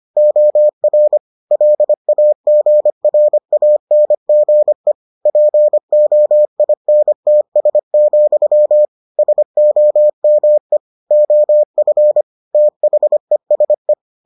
Morse code, heard as a keyed tone, about 25 wpm, 600 Hz. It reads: OR LAGRANGE POINTS, SOME OF THESE